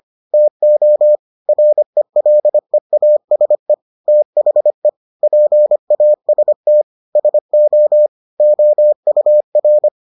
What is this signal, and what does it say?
Morse code, heard as a keyed tone, about 25 wpm, 600 Hz. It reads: TO RELEASE THE PAST SO OUR